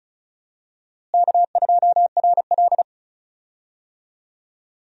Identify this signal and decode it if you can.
Morse code, heard as a keyed tone, about 34 wpm, 700 Hz. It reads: K2RL
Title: USA Call Signs - 4 Characters - Repeated 2x - 35wpm